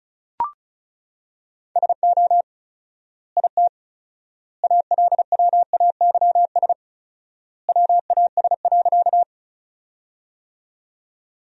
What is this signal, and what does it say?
Morse code, heard as a keyed tone, about 36 wpm, 700 Hz. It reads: SO IT ALWAYS WAS.
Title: 4x Word-Spacing - Sentences from Top 300 Words - 35wpm